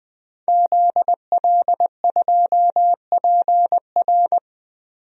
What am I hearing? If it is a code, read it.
Morse code, heard as a keyed tone, about 21 wpm, 700 Hz. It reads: ZL2PR
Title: DXCC-100 Call Signs - 5 Characters - Repeated 3x - 20wpm